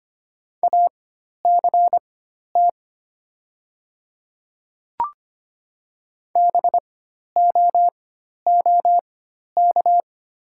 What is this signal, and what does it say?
Morse code, heard as a keyed tone, about 25 wpm, spaced as for 12 wpm, 700 Hz. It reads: ACT BOOK
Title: 4x Character Spacing - Top 300 Words - 25wpm